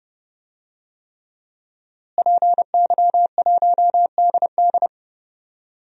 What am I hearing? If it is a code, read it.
Morse code, heard as a keyed tone, about 30 wpm, 700 Hz. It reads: PY1DD
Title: DXCC-100 Call Signs - 5 Characters - Repeated 3x - 30wpm